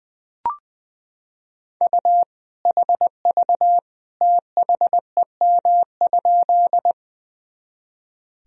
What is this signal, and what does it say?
Morse code, heard as a keyed tone, about 20 wpm, 700 Hz. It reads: U HV THEM?
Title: Sentences from Top 100 Words and Abbreviations 20wpm